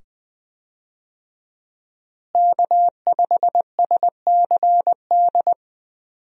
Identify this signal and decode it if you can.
Morse code, heard as a keyed tone, about 21 wpm, 700 Hz. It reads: K5SCD
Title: USA Call Signs - 5 Characters - Repeated 2x - 20wpm